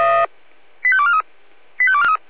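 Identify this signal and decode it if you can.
Five-tone selective call (ZVEI), heard as quick runs of five tones, tones of about 70 ms. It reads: 85243
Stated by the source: ZVEI five-tone selective call 85243, received on an SDR.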